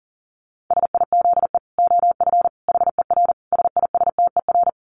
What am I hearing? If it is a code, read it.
Morse code, heard as a keyed tone, about 40 wpm, 700 Hz. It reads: SIZE OF HER SISTER
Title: Dit Challenge 40wpm